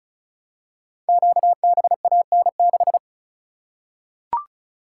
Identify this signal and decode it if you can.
Morse code, heard as a keyed tone, about 36 wpm, 700 Hz. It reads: QBAN6
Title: Code Groups - 5 Letter:Numbers - 35wpm